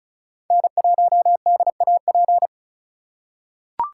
Morse code, heard as a keyed tone, about 34 wpm, 700 Hz. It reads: N1DAP
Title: USA Call Signs - 5 Characters - 2x Character Spacing - 35wpm